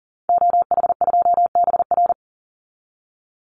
Morse code, heard as a keyed tone, about 37 wpm, 700 Hz. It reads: OH2BR
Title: Call Signs - 40wpm